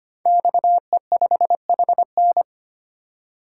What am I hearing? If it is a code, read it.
Morse code, heard as a keyed tone, about 25 wpm, 700 Hz. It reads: XE5HN
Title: Code Groups - 5 Letter:Numbers - 25wpm